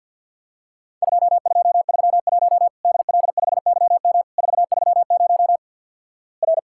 Morse code, heard as a keyed tone, about 50 wpm, 700 Hz. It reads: 2231 DL5YM 430 R
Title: Worked All Europe DX Contest - QTC Exchange with Repeats - 50wpm